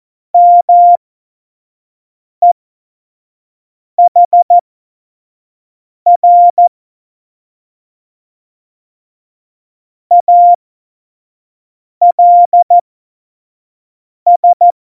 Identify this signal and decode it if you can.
Morse code, heard as a keyed tone, about 14 wpm, spaced as for 5 wpm, 700 Hz. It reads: MEHR ALS
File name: Morsekurs / Witze / Witz 14 Juli 2020, 25 F70